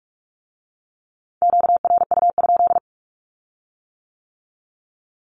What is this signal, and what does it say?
Morse code, heard as a keyed tone, about 46 wpm, 700 Hz. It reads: QRU?
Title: Q-Codes with Spoken Meaning - 45wpm